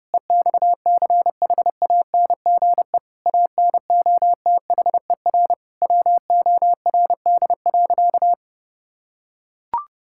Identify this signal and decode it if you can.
Morse code, heard as a keyed tone, about 30 wpm, 700 Hz. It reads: EXCHANGE ANOTHER WORD.
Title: Sentences from Top 1000 Words - Review Part 6 - 30wpm